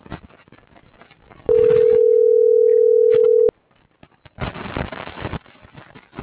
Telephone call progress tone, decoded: ringback tone

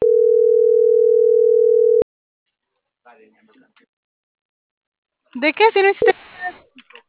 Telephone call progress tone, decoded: ringback tone

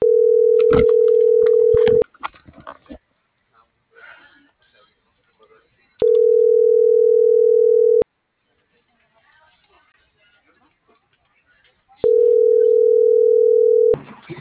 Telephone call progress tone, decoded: ringback tone

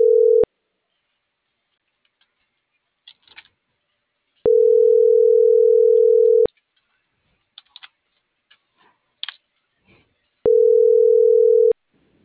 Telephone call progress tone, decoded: ringback tone

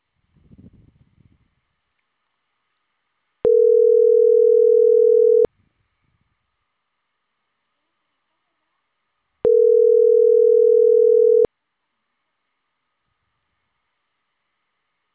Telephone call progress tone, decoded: ringback tone

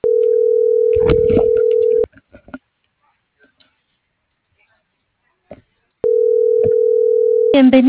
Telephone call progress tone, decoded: ringback tone

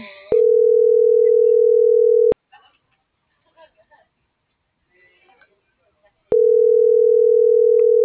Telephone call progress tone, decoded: ringback tone

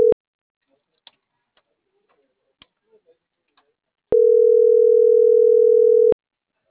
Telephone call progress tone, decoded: ringback tone

